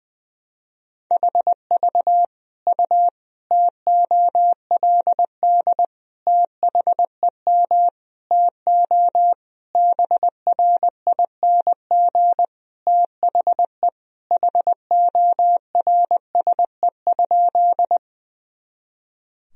Morse code, heard as a keyed tone, about 20 wpm, 700 Hz. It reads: HV U TOLD THEM TO BRING THE HORSE?
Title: Sentences from Top 500 Words and Abbreviations 20wpm